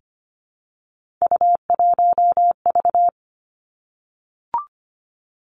Morse code, heard as a keyed tone, about 25 wpm, 700 Hz. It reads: U1V
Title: Three Challenging Letter-Number Combinations 25wpm